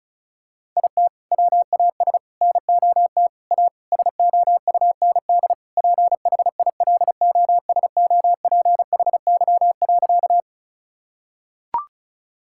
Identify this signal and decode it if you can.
Morse code, heard as a keyed tone, about 36 wpm, 700 Hz. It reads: IT WAS NOT A SOUND PHILOSOPHY.
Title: Sentences from Top 1000 Words - Review Part 6 - 35wpm